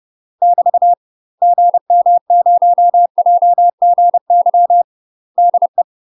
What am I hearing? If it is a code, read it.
Morse code, heard as a keyed tone, about 30 wpm, 700 Hz. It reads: <BT> GM0JGY DE
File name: UK MASTER QSOs TRAINING / 30WPM / UK MASTER QSOs 0014